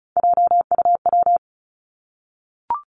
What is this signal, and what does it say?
Morse code, heard as a keyed tone, about 36 wpm, 700 Hz. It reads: JUW
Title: Three Challenging Letter-Number Combinations 35wpm